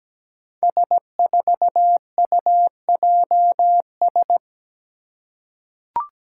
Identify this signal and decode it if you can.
Morse code, heard as a keyed tone, about 17 wpm, 700 Hz. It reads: S4UJS
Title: Five Challenging Letter-Number Combinations 17wpm